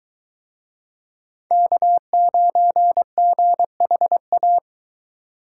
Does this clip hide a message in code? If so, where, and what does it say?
Yes: Morse code from 1.5 to 4.6 s, heard as a keyed tone, about 24 wpm, 700 Hz. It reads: K9GHA